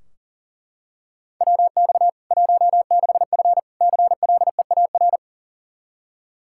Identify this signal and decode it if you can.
Morse code, heard as a keyed tone, about 39 wpm, 700 Hz. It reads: WX 16F CLEAR
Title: K1USN SST QSO Practice 40wpm